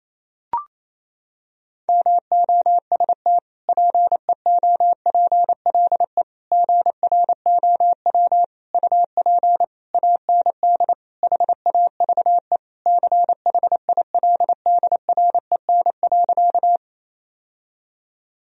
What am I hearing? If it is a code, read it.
Morse code, heard as a keyed tone, about 28 wpm, 700 Hz. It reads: MOST PEOPLE GROW UP AND HAVE CHILDREN.
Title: Sentences from Top 1000 Words - Review Part 1 - 28wpm